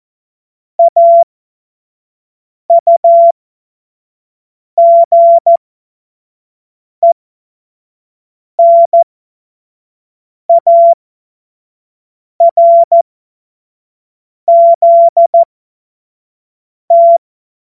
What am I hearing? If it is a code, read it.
Morse code, heard as a keyed tone, about 14 wpm, spaced as for 5 wpm, 680 Hz. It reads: AUGENARZT
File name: Morsekurs / Witze / Witz 26 März 2024 25 F70